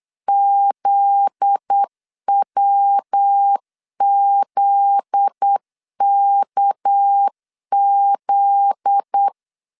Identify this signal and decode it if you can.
Morse code, heard as a keyed tone, about 8 wpm, 800 Hz. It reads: ZWZKZ
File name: LEZIONI / 7 GF9Z / 7 GF9Z 8wpm 3213